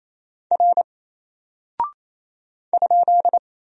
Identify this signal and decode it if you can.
Morse code, heard as a keyed tone, about 28 wpm, 700 Hz. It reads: R ?
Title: Single Character (Letter:Number:Punctuation:Prosign) 28wpm